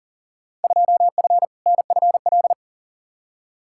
Morse code, heard as a keyed tone, about 40 wpm, 700 Hz. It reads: 2F NFL
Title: Field Day Exchanges - 40wpm